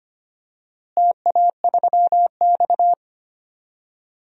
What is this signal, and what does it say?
Morse code, heard as a keyed tone, about 25 wpm, 700 Hz. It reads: TA3X
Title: DXCC-100 Call Signs - 4 Characters - Repeated 3x - 25wpm